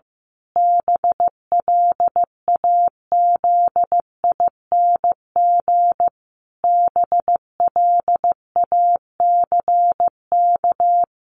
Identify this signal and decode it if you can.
Morse code, heard as a keyed tone, about 15 wpm, 700 Hz. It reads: BLAZING BLACK
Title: A Sound Of Thunder 15wpm